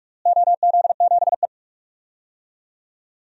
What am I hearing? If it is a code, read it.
Morse code, heard as a keyed tone, about 44 wpm, 700 Hz. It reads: OZ7E
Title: DXCC-100 Call Signs - 4 Characters - Repeated 3x - 45wpm